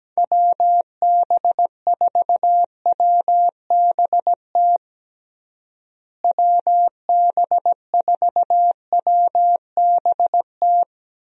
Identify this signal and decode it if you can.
Morse code, heard as a keyed tone, about 17 wpm, 700 Hz. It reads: WB4WBT WB4WBT
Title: USA Call Signs - 6 Characters - Repeated 3x - 17wpm